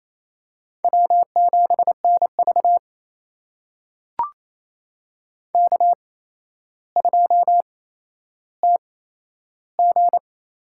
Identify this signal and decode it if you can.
Morse code, heard as a keyed tone, about 27 wpm, spaced as for 12 wpm, 700 Hz. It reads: W7NV K2TG
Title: USA Call Signs - 4 Characters - 8x Character Spacing - 28wpm